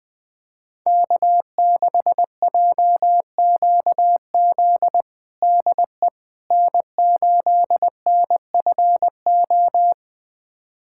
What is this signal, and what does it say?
Morse code, heard as a keyed tone, about 21 wpm, 700 Hz. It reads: K6JQZ DE N8NFO
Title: USA Callsign DE Callsign - 5 Characters - 20wpm